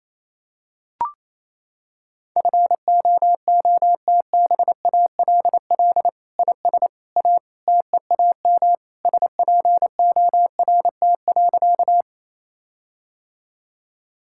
Morse code, heard as a keyed tone, about 28 wpm, 700 Hz. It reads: FOOTBALL IS A TEAM SPORT.
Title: Sentences from Top 1600 Words 28wpm